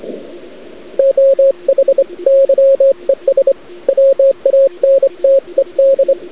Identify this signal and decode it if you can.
Morse code, heard as a keyed tone, about 25 wpm, 540 Hz, with radio noise. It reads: OH YES WANTED
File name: cd-odj / Suara / cwrag